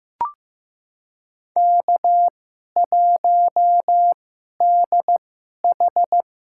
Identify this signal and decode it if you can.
Morse code, heard as a keyed tone, about 15 wpm, spaced as for 11 wpm, 700 Hz. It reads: K1DH
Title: USA Call Signs - 2x Character Spacing - 15wpm